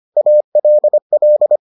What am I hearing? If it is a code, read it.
Morse code, heard as a keyed tone, about 25 wpm, 600 Hz. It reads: ALL